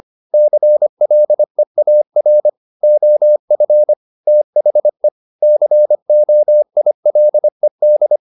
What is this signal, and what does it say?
Morse code, heard as a keyed tone, about 25 wpm, 600 Hz. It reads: CLEAR OF THE COILED
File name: the old man and the sea part 1 25wpm